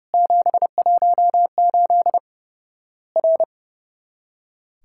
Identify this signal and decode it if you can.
Morse code, heard as a keyed tone, about 30 wpm, 700 Hz. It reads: 718 R
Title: Worked All Europe DX Contest - QTC Exchange - 30wpm